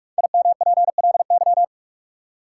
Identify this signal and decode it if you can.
Morse code, heard as a keyed tone, about 46 wpm, 700 Hz. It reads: IMPLY
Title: Five Letter Words - 45wpm